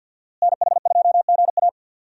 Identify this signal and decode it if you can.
Morse code, heard as a keyed tone, about 52 wpm, 700 Hz. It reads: NH2GA
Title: USA Call Signs - 5 Characters - Repeated 3x - 50wpm